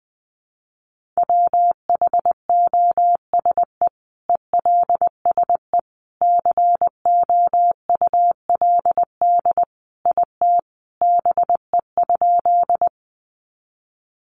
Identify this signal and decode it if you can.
Morse code, heard as a keyed tone, about 20 wpm, 700 Hz. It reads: WHOSE ELSE COULD IT BE?
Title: Sentences from Top 1200 Words 20wpm